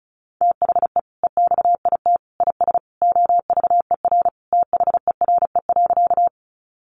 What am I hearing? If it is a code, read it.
Morse code, heard as a keyed tone, about 34 wpm, 700 Hz. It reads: THE EXIT IS OVER THERE.
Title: Sentences from Top 1100 Words 35wpm